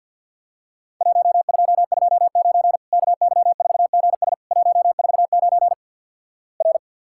Morse code, heard as a keyed tone, about 48 wpm, 700 Hz. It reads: 1229 KY4GS 149 R
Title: Worked All Europe DX Contest - QTC Exchange - 50wpm